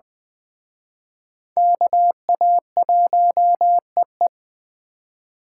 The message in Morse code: KA1EE

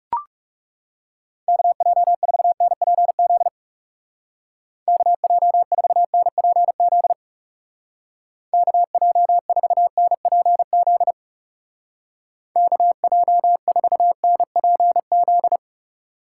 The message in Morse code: KJ4NPZ KJ4NPZ KJ4NPZ KJ4NPZ

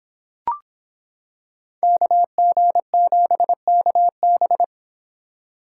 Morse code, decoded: KG7KB

